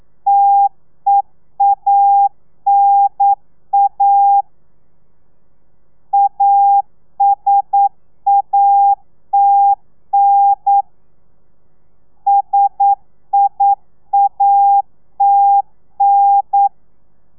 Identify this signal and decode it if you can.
Morse code, heard as a keyed tone, about 9 wpm, 800 Hz. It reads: TEANA ASATN SIATN
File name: telegrafia / i2tfj / Lez2 40